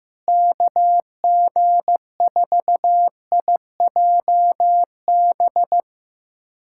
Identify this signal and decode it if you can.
Morse code, heard as a keyed tone, about 15 wpm, 700 Hz. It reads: KG4IJB
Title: USA Callsign DE Callsign - 6 Characters - 15wpm